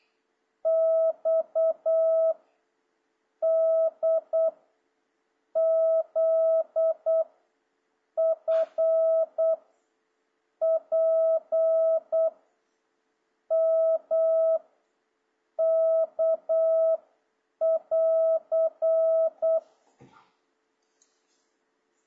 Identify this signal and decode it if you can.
Morse code, heard as a keyed tone, about 8 wpm, 660 Hz. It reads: X D Z F P M K <AR>
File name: jh8cbh / 2026cw